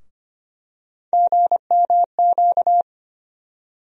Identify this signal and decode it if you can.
Morse code, heard as a keyed tone, about 25 wpm, 700 Hz. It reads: GMQ
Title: USA Call Sign Postfixes - 3 Letters 25wpm